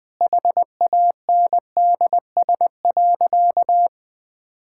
Morse code, heard as a keyed tone, about 20 wpm, 700 Hz. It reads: HANDS.
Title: Sentences with 8-Word Phrases - 20wpm